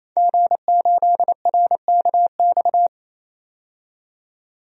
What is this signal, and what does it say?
Morse code, heard as a keyed tone, about 27 wpm, 700 Hz. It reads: G8RKX
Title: DXCC-100 Call Signs - 5 Characters - 28wpm